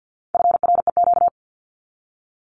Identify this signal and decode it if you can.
Morse code, heard as a keyed tone, about 50 wpm, 700 Hz. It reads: FREQ
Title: Abbreviations 50wpm